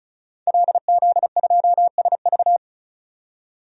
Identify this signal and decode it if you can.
Morse code, heard as a keyed tone, about 34 wpm, 700 Hz. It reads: LZ2SV
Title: DXCC-100 Call Signs - 5 Characters - 35wpm